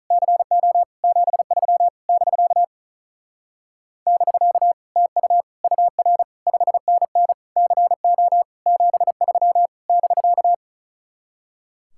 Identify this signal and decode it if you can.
Morse code, heard as a keyed tone, about 46 wpm, 700 Hz. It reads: CO 73 <BK> <BK> TU UR 5NN CO 73 <BK>